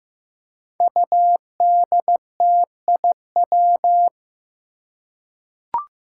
Morse code, heard as a keyed tone, about 15 wpm, 700 Hz. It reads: UDTIW
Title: Five Challenging Letter-Number Combinations 15wpm